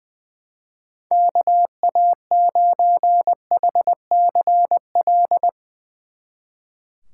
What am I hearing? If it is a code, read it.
Morse code, heard as a keyed tone, about 22 wpm, 700 Hz. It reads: KA9HCL
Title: Speed Racing - USA 2x3 Call Signs 20wpm